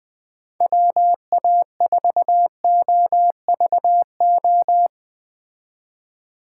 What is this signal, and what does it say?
Morse code, heard as a keyed tone, about 20 wpm, 700 Hz. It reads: WA4OVO
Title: Speed Racing - USA 2x3 Call Signs 17wpm